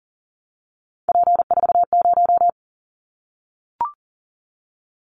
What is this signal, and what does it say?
Morse code, heard as a keyed tone, about 38 wpm, 700 Hz. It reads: P40
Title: Callsign Prefixes for Top 100 DXCC - 40wpm